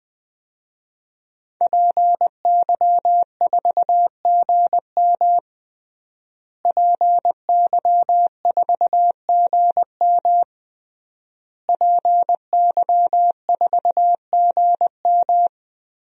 Morse code, heard as a keyed tone, about 20 wpm, 700 Hz. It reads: PY4GM PY4GM PY4GM
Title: DXCC-100 Call Signs - 5 Characters - Repeated 3x - 20wpm